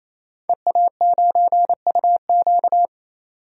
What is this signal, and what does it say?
Morse code, heard as a keyed tone, about 28 wpm, 700 Hz. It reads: EA9UQ
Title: DXCC-100 Call Signs - 5 Characters - 28wpm